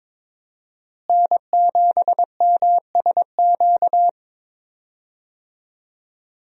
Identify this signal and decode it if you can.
Morse code, heard as a keyed tone, about 22 wpm, 700 Hz. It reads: N7MSQ
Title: USA Call Signs - 5 Characters - Repeated 3x - 22wpm